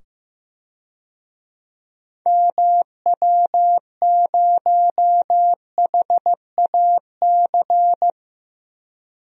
Morse code, heard as a keyed tone, about 15 wpm, 700 Hz. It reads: MW0HAC
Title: DXCC-100 Call Signs - 6 Characters - 15wpm